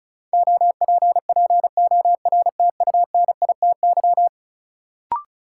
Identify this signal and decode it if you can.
Morse code, heard as a keyed tone, about 36 wpm, 700 Hz. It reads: OPPORTUNITY
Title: Sentences from Top 1000 Words - New Word Drill - 35wpm